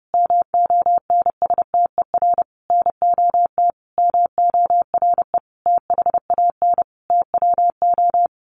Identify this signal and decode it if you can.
Morse code, heard as a keyed tone, about 30 wpm, 700 Hz. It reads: MONSTER NOT MORE THAN TWO
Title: A Sound Of Thunder 30wpm